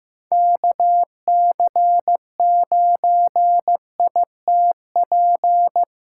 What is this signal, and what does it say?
Morse code, heard as a keyed tone, about 15 wpm, 700 Hz. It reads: KC9ITP
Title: USA Call Signs - 6 Characters - 2x Character Spacing - 15wpm